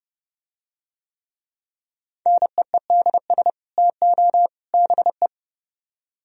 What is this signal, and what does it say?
Morse code, heard as a keyed tone, about 29 wpm, 700 Hz. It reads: NEEDS TO BE